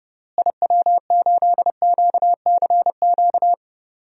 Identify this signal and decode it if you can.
Morse code, heard as a keyed tone, about 30 wpm, 700 Hz. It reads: IW8QCQ
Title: DXCC-100 Call Signs - 6 Characters - Repeated 3x - 30wpm